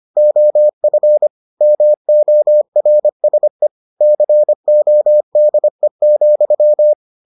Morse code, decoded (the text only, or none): OF MORSE CODE,